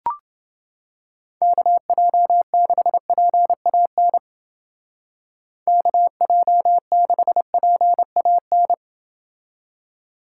Morse code, heard as a keyed tone, about 28 wpm, 700 Hz. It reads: KJ6PAN KJ6PAN